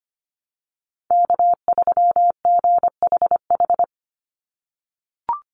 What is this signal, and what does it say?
Morse code, heard as a keyed tone, about 25 wpm, 700 Hz. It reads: K3GHH
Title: Call Signs - 25wpm